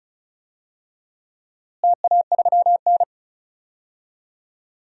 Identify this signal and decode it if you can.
Morse code, heard as a keyed tone, about 36 wpm, 700 Hz. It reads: TA3N
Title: DXCC-100 Call Signs - 4 Characters - Repeated 2x - 35wpm